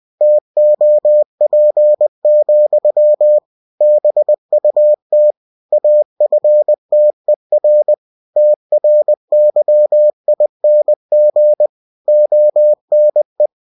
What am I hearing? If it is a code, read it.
Morse code, heard as a keyed tone, about 20 wpm, 600 Hz. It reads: TOP, BUT AFTER TRYING ONE
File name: cw-books / winnie-the-pooh / winnie-20wpm-page-0095